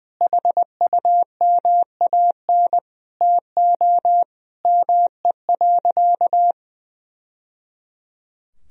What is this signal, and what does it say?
Morse code, heard as a keyed tone, about 20 wpm, 700 Hz. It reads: HUMAN TO ME.
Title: Sentences from Top 1000 Words - Review Part 3 - 20wpm